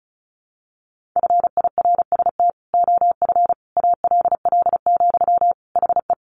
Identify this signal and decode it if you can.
Morse code, heard as a keyed tone, about 34 wpm, 700 Hz. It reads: FIRST OF ALL, HE
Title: Sentences from Top 1200 Words 35wpm